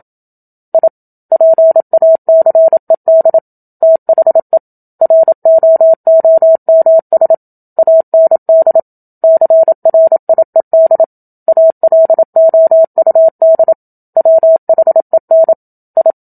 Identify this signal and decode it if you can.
Morse code, heard as a keyed tone, about 27 wpm, 650 Hz. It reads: I PACED THE ROOMS AND CRIED ALOUD WHEN I
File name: Book Two / 01-Chapter 1 - 27 WPM